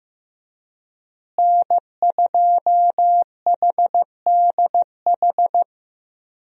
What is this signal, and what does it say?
Morse code, heard as a keyed tone, about 15 wpm, 700 Hz. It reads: N2HDH
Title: USA Call Signs - 5 Characters - Repeated 2x - 15wpm